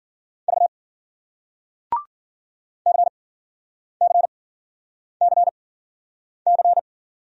Morse code, heard as a keyed tone, about 56 wpm, spaced as for 9 wpm, 700 Hz. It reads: V CCCC